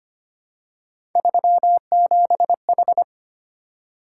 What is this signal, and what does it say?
Morse code, heard as a keyed tone, about 25 wpm, 700 Hz. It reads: 37H
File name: Three Letter-Number Combinations - 25wpm